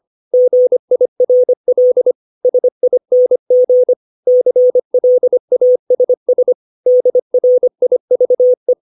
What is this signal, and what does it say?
Morse code, heard as a keyed tone, about 25 wpm, 500 Hz. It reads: GIRL SING CLASS DRIVE
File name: 632 words / 632 Words 25WPM